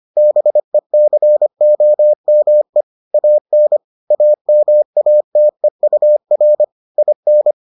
Morse code, heard as a keyed tone, about 25 wpm, 600 Hz. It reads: BECOME AN AMATEUR IN